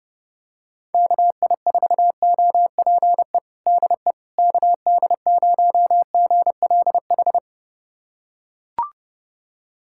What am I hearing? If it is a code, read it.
Morse code, heard as a keyed tone, about 32 wpm, 700 Hz. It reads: KI4OPE DE KD0GLH
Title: USA Callsign DE Callsign 30wpm